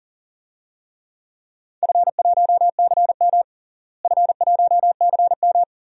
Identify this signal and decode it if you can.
Morse code, heard as a keyed tone, about 40 wpm, 700 Hz. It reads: F1CM F1CM
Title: DXCC-100 Call Signs - 4 Characters - Repeated 2x - 40wpm